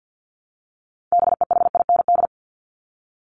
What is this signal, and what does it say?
Morse code, heard as a keyed tone, about 50 wpm, 700 Hz. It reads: BEHIND
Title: Top 500 Words 50wpm